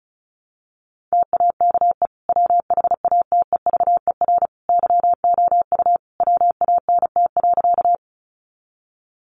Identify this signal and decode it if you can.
Morse code, heard as a keyed tone, about 34 wpm, 700 Hz. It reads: TAKE WHATEVER YOU WANT.
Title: Sentences from Top 1100 Words 35wpm